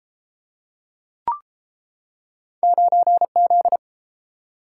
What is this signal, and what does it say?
Morse code, heard as a keyed tone, about 32 wpm, 700 Hz. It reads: 9Z